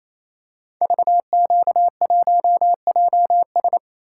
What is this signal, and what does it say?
Morse code, heard as a keyed tone, about 27 wpm, 700 Hz. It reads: VQ1JS
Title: Code Groups - 5 Letter:Numbers - 28wpm